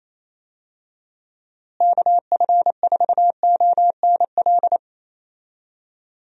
Morse code, heard as a keyed tone, about 28 wpm, 700 Hz. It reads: KF4ONL
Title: USA Call Signs - 6 Characters - Repeated 2x - 28wpm